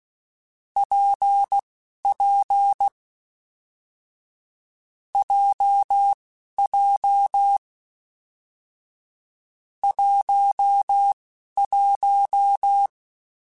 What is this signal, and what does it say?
Morse code, heard as a keyed tone, about 16 wpm, spaced as for 11 wpm, 800 Hz. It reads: PP JJ 11
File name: Difficult Combos / Difficult Combos-mp3s / WPJ WPJ1 @10